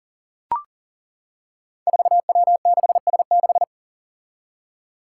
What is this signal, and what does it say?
Morse code, heard as a keyed tone, about 40 wpm, 700 Hz. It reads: 4W6S6
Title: Five Challenging Letter-Number Combinations 40wpm